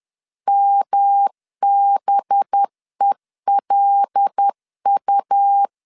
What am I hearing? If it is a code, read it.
Morse code, heard as a keyed tone, about 10 wpm, 800 Hz. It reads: MBELU
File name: LEZIONI / 6 PKX0 / 6 P 10wpm 2432